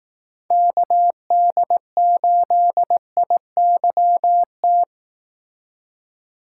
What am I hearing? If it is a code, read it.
Morse code, heard as a keyed tone, about 18 wpm, 700 Hz. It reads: KD8IYT
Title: Speed Racing - USA Call Signs 15wpm